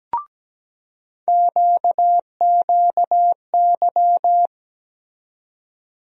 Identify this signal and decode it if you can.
Morse code, heard as a keyed tone, about 17 wpm, 700 Hz. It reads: QQY
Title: USA Call Sign Postfixes - 3 Letters 17wpm